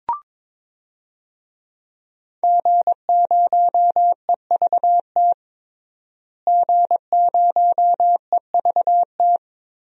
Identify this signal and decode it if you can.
Morse code, heard as a keyed tone, about 22 wpm, 700 Hz. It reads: G0EVT G0EVT
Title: DXCC-100 Call Signs - 5 Characters - Repeated 3x - 22wpm